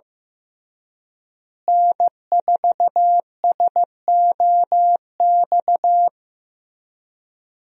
Morse code, heard as a keyed tone, about 15 wpm, 700 Hz. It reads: N4SOX
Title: USA Call Signs - 5 Characters - Repeated 3x - 15wpm